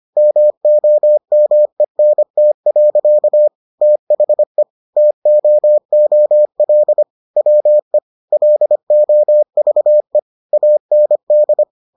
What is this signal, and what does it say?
Morse code, heard as a keyed tone, about 25 wpm, 600 Hz. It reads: MOMENT. THE TOOL WE LOVE AND